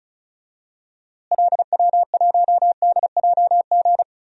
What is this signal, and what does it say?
Morse code, heard as a keyed tone, about 34 wpm, 700 Hz. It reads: LW1DJG
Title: DXCC-100 Call Signs - 6 Characters - Repeated 2x - 35wpm